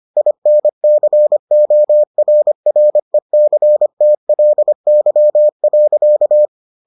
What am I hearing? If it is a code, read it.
Morse code, heard as a keyed tone, about 25 wpm, 600 Hz. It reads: INCORRECTLY.